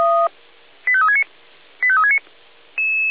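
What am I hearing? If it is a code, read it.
Five-tone selective call (ZVEI), heard as quick runs of five tones, tones of about 70 ms. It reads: 85369 85369
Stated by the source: ZVEI five-tone selective call received on SDR.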